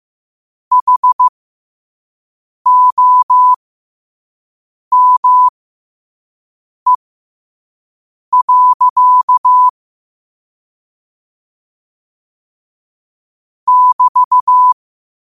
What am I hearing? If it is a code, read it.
Morse code, heard as a keyed tone, about 15 wpm, spaced as for 5 wpm, 1000 Hz. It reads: HOME. <BT>
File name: qotdfiles / 5wpm